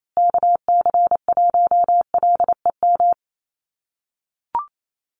Morse code, heard as a keyed tone, about 28 wpm, 700 Hz. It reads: KC1LEM
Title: USA Call Signs Repeated 3x 28wpm